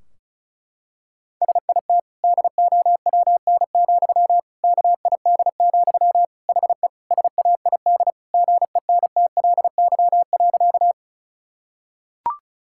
Morse code, heard as a keyed tone, about 36 wpm, 700 Hz. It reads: SIT DOWN, KID, HE SAID GENTLY.
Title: Sentences from Top 1600 Words 35wpm